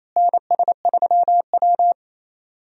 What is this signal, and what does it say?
Morse code, heard as a keyed tone, about 28 wpm, 700 Hz. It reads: NS3W
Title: USA Call Signs - 4 Characters - Repeated 3x - 28wpm